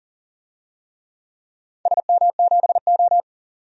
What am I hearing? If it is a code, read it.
Morse code, heard as a keyed tone, about 40 wpm, 700 Hz. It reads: SM7O
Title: DXCC-100 Call Signs - 4 Characters - 40wpm